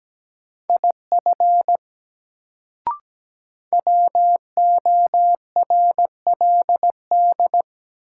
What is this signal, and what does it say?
Morse code, heard as a keyed tone, about 17 wpm, 700 Hz. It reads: IF WORLD
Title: Sets of 2 Words - New Word Drill - 17wpm